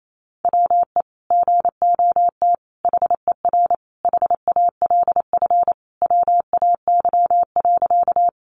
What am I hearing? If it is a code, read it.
Morse code, heard as a keyed tone, about 28 wpm, 700 Hz. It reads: WE GOT HER HALF WAY.
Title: Sentences from Top 400 Words 28wpm